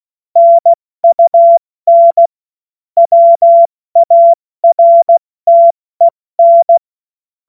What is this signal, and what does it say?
Morse code, heard as a keyed tone, about 16 wpm, spaced as for 14 wpm, 680 Hz. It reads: NUN WARTEN
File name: Witz-23-September-2025-70-F80